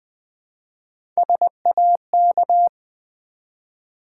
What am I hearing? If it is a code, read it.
Morse code, heard as a keyed tone, about 20 wpm, 700 Hz. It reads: SAK
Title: Build A Word - 1 Letter At A Time - 4-Letter Words - 20wpm